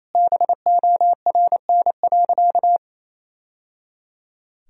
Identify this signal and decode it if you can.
Morse code, heard as a keyed tone, about 28 wpm, 700 Hz. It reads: BORN.
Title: Sentences from Top 1000 Words - Review Part 2 - 28wpm